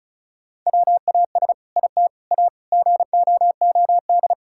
Morse code, heard as a keyed tone, about 36 wpm, 700 Hz. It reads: WAS IT A GOOD
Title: Sentences from Top 1000 Words - Review Part 5 - 35wpm